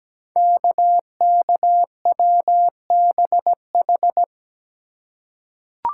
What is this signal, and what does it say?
Morse code, heard as a keyed tone, about 17 wpm, 700 Hz. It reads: KKWBH
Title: Code Groups - 5 Letters - 17wpm